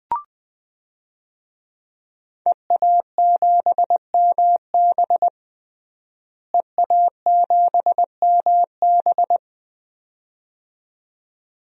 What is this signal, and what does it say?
Morse code, heard as a keyed tone, about 20 wpm, 700 Hz. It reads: EA7MB EA7MB
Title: DXCC-100 Call Signs - 5 Characters - Repeated 2x - 20wpm